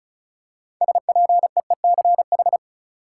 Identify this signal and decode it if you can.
Morse code, heard as a keyed tone, about 34 wpm, 700 Hz. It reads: SPEECH
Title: Six Letter Words - 35wpm